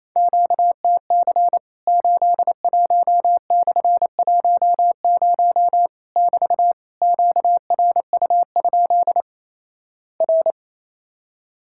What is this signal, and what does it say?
Morse code, heard as a keyed tone, about 28 wpm, 700 Hz. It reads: QTC 81/10 <BT> QRU? R
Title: Worked All Europe DX Contest - QTC Exchange - 28wpm